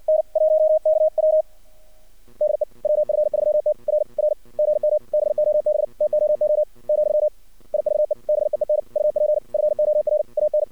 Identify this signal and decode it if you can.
Morse code, heard as a keyed tone, about 59 wpm, 610 Hz. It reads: N1KW DE WF3T R R GM BOB TOO , IVE BEEN UP FOR AN
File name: archived / CRSnet